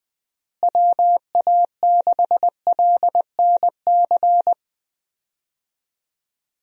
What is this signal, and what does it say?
Morse code, heard as a keyed tone, about 20 wpm, 700 Hz. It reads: WA6LNC
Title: USA Call Signs - 6 Characters - Repeated 3x - 20wpm